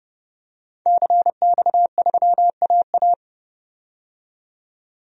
Morse code, heard as a keyed tone, about 32 wpm, 700 Hz. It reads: CX3AA
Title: DXCC-100 Call Signs - 5 Characters - 30wpm